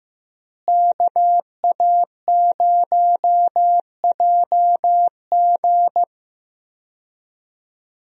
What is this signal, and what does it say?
Morse code, heard as a keyed tone, about 15 wpm, 700 Hz. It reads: KA0JG
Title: USA Call Signs - 5 Characters - Repeated 3x - 15wpm